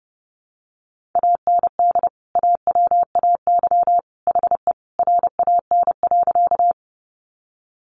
Morse code, heard as a keyed tone, about 28 wpm, 700 Hz. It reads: AND AWAY HE RAN.
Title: Sentences from Top 600 Words 30wpm